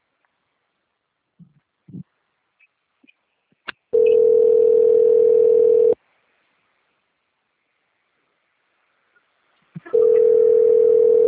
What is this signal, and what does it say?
Ringback tone (North America), 440 + 480 Hz.